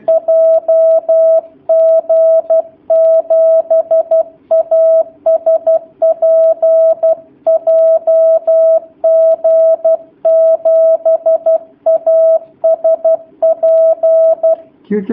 Morse code, heard as a keyed tone, about 12 wpm, 650 Hz. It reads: JG7ASP JG7ASP
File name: jh8cbh / 2019cw / sp11- w